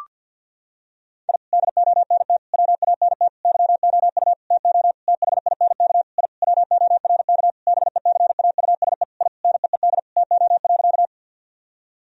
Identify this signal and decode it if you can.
Morse code, heard as a keyed tone, about 47 wpm, 700 Hz. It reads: I DONT WANT YOU TO THINK I WORK BECAUSE I NEED TO.
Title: Sentences from Top 1000 Words - Review Part 3 - 50wpm